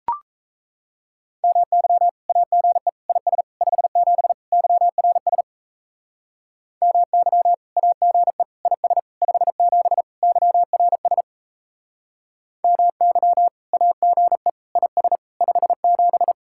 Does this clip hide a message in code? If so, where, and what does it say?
Yes: Morse code from 1.4 to 16.3 s, heard as a keyed tone, about 39 wpm, 700 Hz. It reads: MY AGE IS 57 YRS MY AGE IS 57 YRS MY AGE IS 57